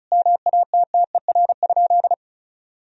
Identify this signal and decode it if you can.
Morse code, heard as a keyed tone, about 36 wpm, 700 Hz. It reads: MATTER?
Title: Sentences from Top 1000 Words - Review Part 4 - 35wpm